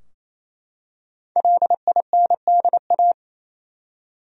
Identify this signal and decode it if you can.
Morse code, heard as a keyed tone, about 27 wpm, 700 Hz. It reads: LINDA
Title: USA Common Names - Max 5 Letters - 28wpm